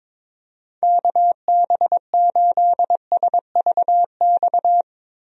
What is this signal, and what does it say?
Morse code, heard as a keyed tone, about 21 wpm, 700 Hz. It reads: KB8SVX
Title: USA Call Signs - 6 Characters - Repeated 2x - 22wpm